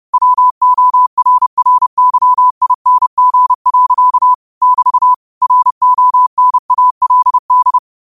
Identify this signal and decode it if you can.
Morse code, heard as a keyed tone, about 30 wpm, 1000 Hz. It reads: WORRYING. <BT> RONALD
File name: qotdfiles / 30wpm